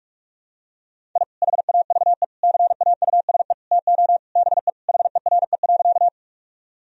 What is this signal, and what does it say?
Morse code, heard as a keyed tone, about 46 wpm, 700 Hz. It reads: I HAVE CAUSE TO BE HERE.
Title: Sentences from Top 1000 Words - Review Part 1 - 45wpm